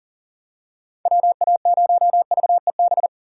Morse code, heard as a keyed tone, about 40 wpm, 700 Hz. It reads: WA0VEB
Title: USA Call Signs - 6 Characters - 3x Character Spacing - 40wpm